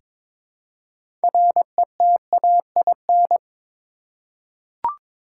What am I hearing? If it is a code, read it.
Morse code, heard as a keyed tone, about 22 wpm, 700 Hz. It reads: RETAIN
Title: Six Letter Words - 22wpm